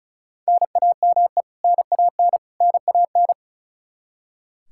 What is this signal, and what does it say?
Morse code, heard as a keyed tone, about 36 wpm, 700 Hz. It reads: NAME NAN NAN
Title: Practice QSO - Exchange 1 - Beginning - 35wpm